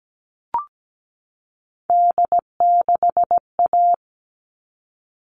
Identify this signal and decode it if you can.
Morse code, heard as a keyed tone, about 17 wpm, 700 Hz. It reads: D6A
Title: All Cut Numbers - 3-Digit 17wpm